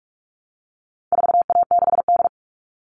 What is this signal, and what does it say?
Morse code, heard as a keyed tone, about 44 wpm, 700 Hz. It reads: 4A6D
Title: All Cut Numbers - 4-Digit 45wpm